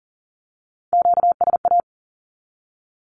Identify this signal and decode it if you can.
Morse code, heard as a keyed tone, about 40 wpm, 700 Hz. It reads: QSA
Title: Q-Codes 40wpm